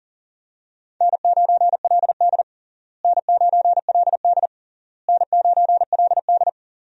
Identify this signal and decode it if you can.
Morse code, heard as a keyed tone, about 40 wpm, 700 Hz. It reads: N9LD N9LD N9LD
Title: USA Call Signs - 4 Characters - Repeated 3x - 40wpm